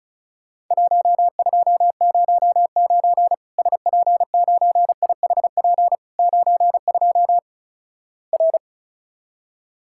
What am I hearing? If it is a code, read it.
Morse code, heard as a keyed tone, about 36 wpm, 700 Hz. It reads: 1209 SP9IHP 92 R
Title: Worked All Europe DX Contest - QTC Exchange - 35wpm